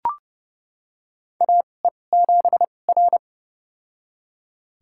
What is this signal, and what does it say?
Morse code, heard as a keyed tone, about 30 wpm, spaced as for 22 wpm, 700 Hz. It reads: AE7R
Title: USA Call Signs - 4 Characters - 2x Character Spacing - 30wpm